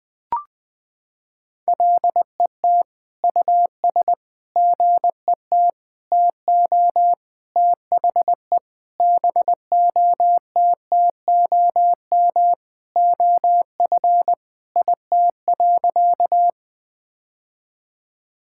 Morse code, heard as a keyed tone, about 20 wpm, 700 Hz. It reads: LET US GET TO THE BOTTOM OF IT.
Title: Sentences from Top 1000 Words - Review Part 6 - 20wpm